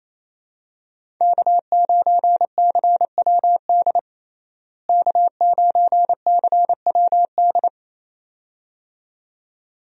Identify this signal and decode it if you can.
Morse code, heard as a keyed tone, about 27 wpm, 700 Hz. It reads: K9CWD K9CWD
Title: USA Call Signs - 5 Characters - Repeated 2x - 28wpm